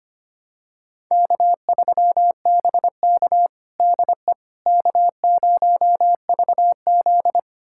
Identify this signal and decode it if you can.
Morse code, heard as a keyed tone, about 25 wpm, 700 Hz. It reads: K3BK DE K0VZ
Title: USA Callsign DE Callsign - 4 Characters - 25wpm